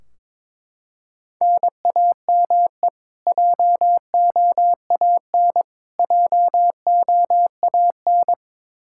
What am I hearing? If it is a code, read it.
Morse code, heard as a keyed tone, about 22 wpm, 700 Hz. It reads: NAME JOAN JOAN
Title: Practice QSO - Exchange 1 - Beginning - 22wpm